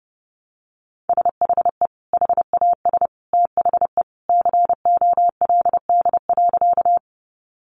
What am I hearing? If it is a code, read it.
Morse code, heard as a keyed tone, about 30 wpm, 700 Hz. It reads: SHE HAS THE COLD.
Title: Easy Phrases 30wpm